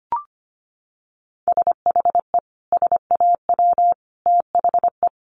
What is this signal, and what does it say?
Morse code, heard as a keyed tone, about 25 wpm, 700 Hz. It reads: SHE SAW THE
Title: Sentences from Top 1200 Words 25wpm